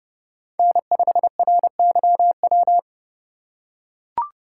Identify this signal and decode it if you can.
Morse code, heard as a keyed tone, about 29 wpm, 700 Hz. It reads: N5RYW
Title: USA Call Signs - 5 Characters - 3x Character Spacing - 30wpm